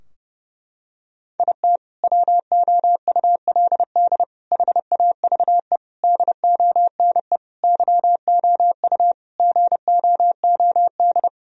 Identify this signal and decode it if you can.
Morse code, heard as a keyed tone, about 30 wpm, 700 Hz. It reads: IT WOULD HAVE DONE YOU GOOD
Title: Sentences from Top 1000 Words - Review Part 2 - 30wpm